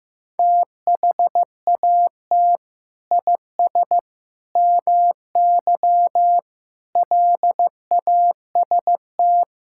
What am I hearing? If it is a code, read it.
Morse code, heard as a keyed tone, about 15 wpm, 700 Hz. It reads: THAT IS MY LAST
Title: Sentences from Top 1600 Words 15wpm